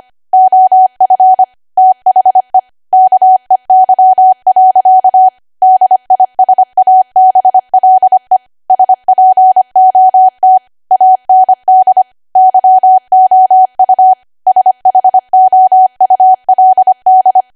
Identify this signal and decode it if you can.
Morse code, heard as a keyed tone, about 25 wpm, 750 Hz. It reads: OF THE KEY. DISABLE SPOT AND YOU SHOULD
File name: Archive / 25 WPM / 25WPM